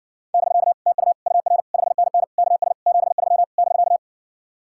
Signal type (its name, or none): Morse code